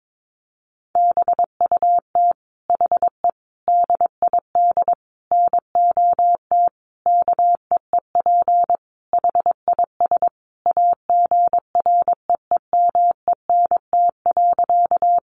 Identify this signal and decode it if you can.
Morse code, heard as a keyed tone, about 22 wpm, 700 Hz. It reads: BUT HE DID NOT KEEP HIS AGREEMENT.
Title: Sentences from Top 1500 Words 22wpm